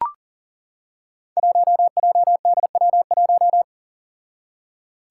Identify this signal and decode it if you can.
Morse code, heard as a keyed tone, about 42 wpm, 700 Hz. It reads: 1JDW1
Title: Five Challenging Letter-Number Combinations 40wpm